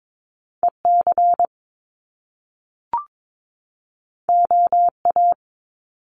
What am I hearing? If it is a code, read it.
Morse code, heard as a keyed tone, about 22 wpm, 700 Hz. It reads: EC OA
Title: Top Two Letter Combinations in Words 22wpm